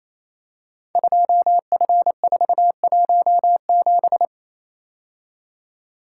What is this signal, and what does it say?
Morse code, heard as a keyed tone, about 27 wpm, 700 Hz. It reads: 2F417
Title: Code Groups - 5 Letter:Numbers - 28wpm